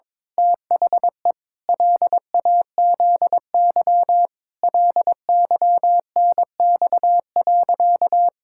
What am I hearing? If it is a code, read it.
Morse code, heard as a keyed tone, about 22 wpm, 700 Hz. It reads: THE LAZY LYNX.
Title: Pangram Challenge v2 - 22wpm